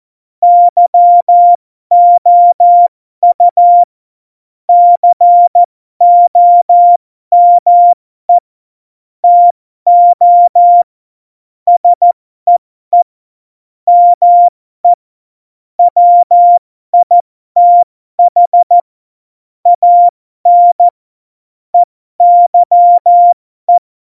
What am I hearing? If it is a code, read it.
Morse code, heard as a keyed tone, about 14 wpm, spaced as for 12 wpm, 700 Hz. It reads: YOU COME TO SEE ME WITH AN EYE
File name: Morsekurs / Witze